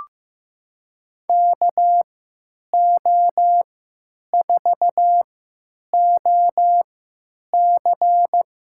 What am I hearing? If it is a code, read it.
Morse code, heard as a keyed tone, about 15 wpm, spaced as for 9 wpm, 700 Hz. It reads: KO4OC
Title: USA Call Signs - 3x Character Spacing - 15wpm